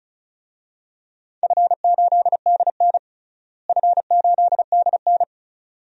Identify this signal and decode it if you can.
Morse code, heard as a keyed tone, about 36 wpm, 700 Hz. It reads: F8DN F8DN
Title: DXCC-100 Call Signs - 4 Characters - Repeated 2x - 35wpm